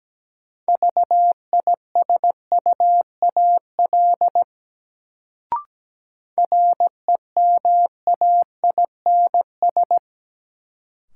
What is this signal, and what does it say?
Morse code, heard as a keyed tone, about 17 wpm, 700 Hz. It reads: VISUAL REMAINS
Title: Sets of 5 Words Encore - New Word Drill - 17wpm